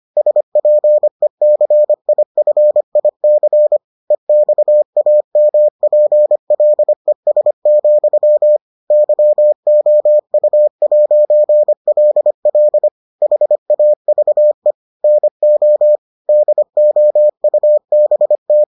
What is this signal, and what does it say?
Morse code, heard as a keyed tone, about 25 wpm, 600 Hz. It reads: SPECIFIC EXAMPLES, YOU'LL HAVE NO DOUBT